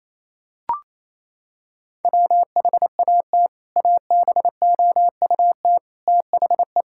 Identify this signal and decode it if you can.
Morse code, heard as a keyed tone, about 28 wpm, 700 Hz. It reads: WHAT ABOUT THE
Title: Sentences from Top 1600 Words 28wpm